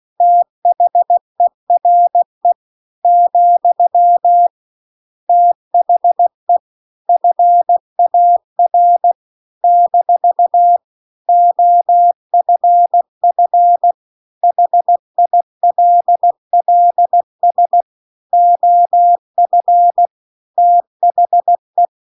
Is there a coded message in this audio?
Yes: Morse code from 0.2 to 21.9 s, heard as a keyed tone, about 16 wpm, 700 Hz. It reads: THERE , THE FAR - OFF HILLS OF THE